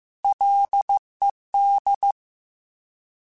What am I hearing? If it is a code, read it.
Morse code, heard as a keyed tone, about 15 wpm, 800 Hz. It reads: LED
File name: F and L / F and L mp3 / F and L Prefixes 3x2 Twice @13